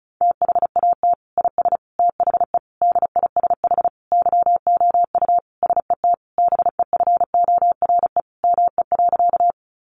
Morse code, heard as a keyed tone, about 34 wpm, 700 Hz. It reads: THAT IS THE DISH YOU SET BEFORE ME.
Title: Sentences from Top 1300 Words 35wpm